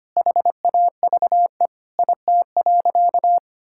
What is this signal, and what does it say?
Morse code, heard as a keyed tone, about 25 wpm, 700 Hz. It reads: HAVE IT.
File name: practice / 25wpm / Sentences with 3-Word Phrases - 25wpm